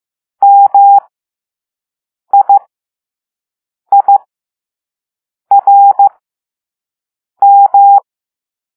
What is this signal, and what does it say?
Morse code, heard as a keyed tone, about 15 wpm, spaced as for 5 wpm, 800 Hz. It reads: MIIRM
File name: Morse / Level2A